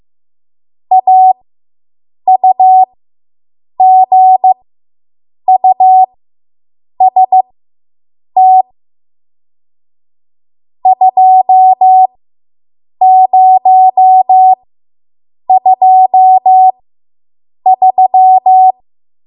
Morse code, heard as a keyed tone, about 15 wpm, spaced as for 7 wpm, 750 Hz. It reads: AUGUST 2023